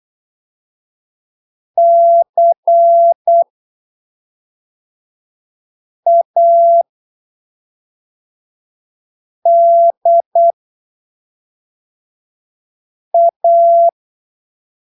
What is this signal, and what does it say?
Morse code, heard as a keyed tone, about 8 wpm, spaced as for 3 wpm, 680 Hz. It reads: CADA
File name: arquivos / texto / texto 20 08ppm